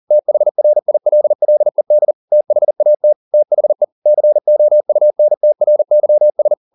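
Morse code, heard as a keyed tone, about 40 wpm, 600 Hz. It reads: THRILLED THAT THE COUNTRYS
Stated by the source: Morse code at 40 wpm, practice text, CW med Gnister.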